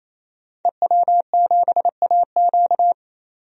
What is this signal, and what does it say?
Morse code, heard as a keyed tone, about 27 wpm, 700 Hz. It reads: EW7AQ
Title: DXCC-100 Call Signs - 5 Characters - 28wpm